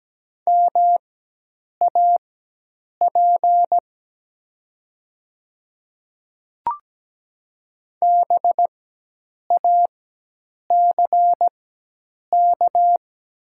Morse code, heard as a keyed tone, about 17 wpm, spaced as for 8 wpm, 700 Hz. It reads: MAP BACK
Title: 4x Character Spacing - Top 500 Words - 17wpm